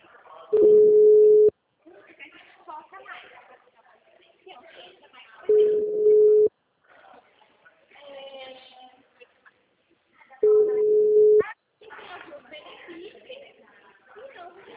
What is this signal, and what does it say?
Ringback tone (Europe, 425 Hz), with speech.